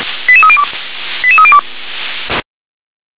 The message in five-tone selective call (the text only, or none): 88302 88302